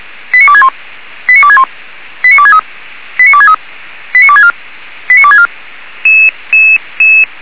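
Five-tone selective call (ZVEI), heard as quick runs of five tones, tones of about 80 ms. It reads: 89261 89261 89263 89263 89264 89264
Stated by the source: ZVEI five-tone selective call 89269, received on SDR.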